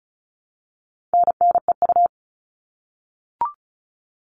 Morse code, heard as a keyed tone, about 34 wpm, 700 Hz. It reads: NNEU
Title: All Cut Numbers - 4-Digit 35wpm